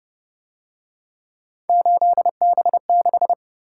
Morse code, heard as a keyed tone, about 28 wpm, 700 Hz. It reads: 8B6